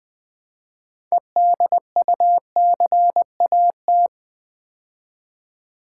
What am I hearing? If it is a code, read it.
Morse code, heard as a keyed tone, about 20 wpm, 700 Hz. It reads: EDUCAT
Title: Build A Word - 3 Letters At A Time - 8-Letter Words - 20wpm